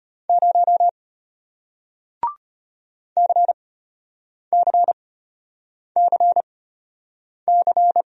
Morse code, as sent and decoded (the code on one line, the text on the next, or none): ----- / -.-. -.-. -.-. -.-.
0 CCCC